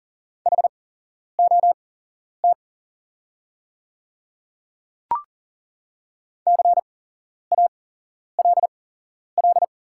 Morse code, heard as a keyed tone, about 40 wpm, spaced as for 11 wpm, 700 Hz. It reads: HOT CALL
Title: 8x Character Spacing - Top 200 Words - 40wpm